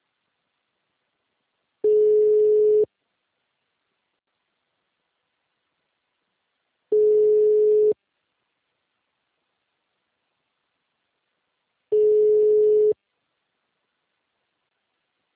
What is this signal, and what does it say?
Ringback tone (Europe, 425 Hz).
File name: out- -2013- - - .47932